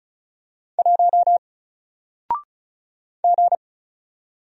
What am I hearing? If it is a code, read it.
Morse code, heard as a keyed tone, about 35 wpm, 700 Hz. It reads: 1 G
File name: Single-Letter-Number-35wpm